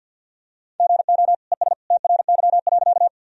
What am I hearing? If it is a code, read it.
Morse code, heard as a keyed tone, about 51 wpm, 700 Hz. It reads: GO ES TRY.
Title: Sentences from Top 200 Words and Abbreviations 50wpm